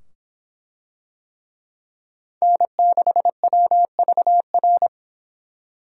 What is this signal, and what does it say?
Morse code, heard as a keyed tone, about 25 wpm, 700 Hz. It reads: N6WVR